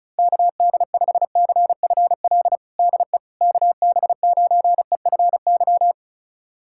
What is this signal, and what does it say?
Morse code, heard as a keyed tone, about 34 wpm, 690 Hz. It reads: KD5CFL DE KB9EFY
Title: USA Callsign DE Callsign 35wpm